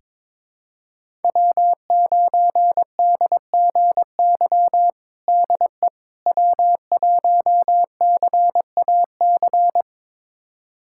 Morse code, heard as a keyed tone, about 22 wpm, 700 Hz. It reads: W9DGY DE W1CAC
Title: USA Callsign DE Callsign - 5 Characters - 22wpm